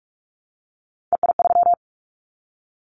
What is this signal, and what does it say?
Morse code, heard as a keyed tone, about 44 wpm, 700 Hz. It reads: EI3
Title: Callsign Prefixes for Top 50 DXCC - 45wpm